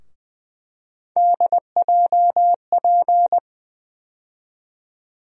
Morse code, heard as a keyed tone, about 20 wpm, 700 Hz. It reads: DJP
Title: USA Call Sign Postfixes - 3 Letters 20wpm